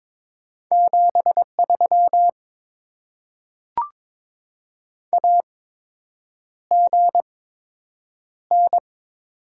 Morse code, heard as a keyed tone, about 22 wpm, spaced as for 8 wpm, 700 Hz. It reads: 73 AGN
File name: Top 20 QSO Elements - 8x Character Spacing - 22wpm